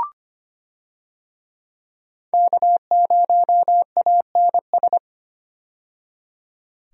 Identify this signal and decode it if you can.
Morse code, heard as a keyed tone, about 25 wpm, 700 Hz. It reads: K0ANS
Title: USA Call Signs - 5 Characters - 25wpm